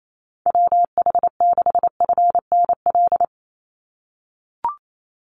Morse code, heard as a keyed tone, about 28 wpm, 700 Hz. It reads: WH6FNL
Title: USA Call Signs 28wpm